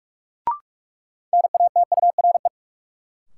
Morse code, heard as a keyed tone, about 43 wpm, 700 Hz. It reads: NATURE
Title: Sets of 3 Words - New Word Drill - 45wpm